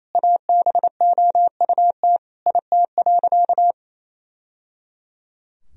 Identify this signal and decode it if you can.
Morse code, heard as a keyed tone, about 28 wpm, 700 Hz. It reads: ABOUT IT.
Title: Sentences from Top 1000 Words - Review Part 2 - 28wpm